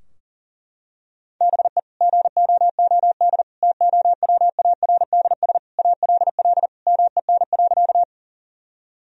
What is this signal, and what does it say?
Morse code, heard as a keyed tone, about 40 wpm, 700 Hz. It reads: BE GOOD TOWARDS ALL MEN.
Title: Sentences from Top 1000 Words - Review Part 2 - 40wpm